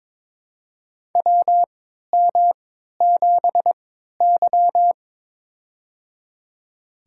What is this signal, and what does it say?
Morse code, heard as a keyed tone, about 22 wpm, spaced as for 12 wpm, 700 Hz. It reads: WM7Y
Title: USA Call Signs - 4 Characters - 3x Character Spacing - 22wpm